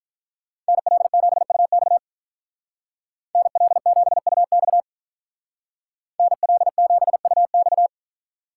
Morse code, heard as a keyed tone, about 45 wpm, 700 Hz. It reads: NL7UX NL7UX NL7UX